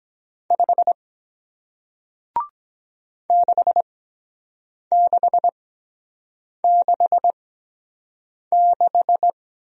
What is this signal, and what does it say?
Morse code, heard as a keyed tone, about 23 wpm, spaced as for 7 wpm, 700 Hz. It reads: 5 6666